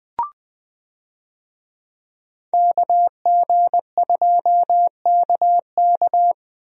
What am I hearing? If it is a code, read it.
Morse code, heard as a keyed tone, about 21 wpm, 700 Hz. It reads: KG2KK
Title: USA Call Signs - 5 Characters - Repeated 3x - 20wpm